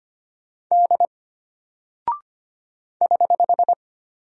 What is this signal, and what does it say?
Morse code, heard as a keyed tone, about 25 wpm, 700 Hz. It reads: D <HH>
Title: Single Character (Letter:Number:Punctuation:Prosign) Alt 25wpm